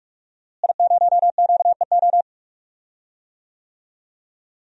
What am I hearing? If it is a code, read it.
Morse code, heard as a keyed tone, about 45 wpm, 700 Hz. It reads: I0QEO
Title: DXCC-100 Call Signs - 5 Characters - Repeated 3x - 45wpm